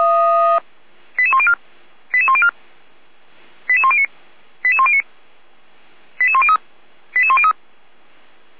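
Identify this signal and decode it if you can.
Five-tone selective call (ZVEI), heard as quick runs of five tones, tones of about 70 ms. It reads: 88104 88104 88109 88109 88103 88103
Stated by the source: ZVEI five-tone selective call received on an SDR.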